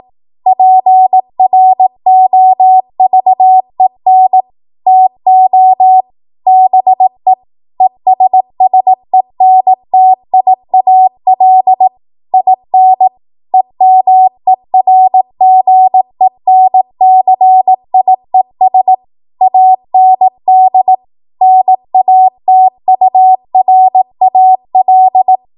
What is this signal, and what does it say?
Morse code, heard as a keyed tone, about 18 wpm, 750 Hz. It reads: PROVEN TO BE ESSENTIAL IN EMERGENCIES AND NATURAL
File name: Morse / 18WPM